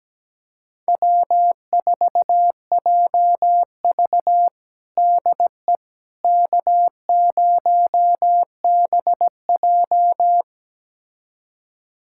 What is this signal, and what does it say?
Morse code, heard as a keyed tone, about 17 wpm, 700 Hz. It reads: W4JV DE K0BJ
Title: USA Callsign DE Callsign - 4 Characters - 17wpm